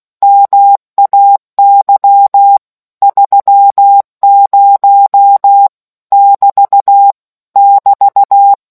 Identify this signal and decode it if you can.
Morse code, heard as a keyed tone, about 16 wpm, 800 Hz. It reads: MAY 30 <BT> <BT>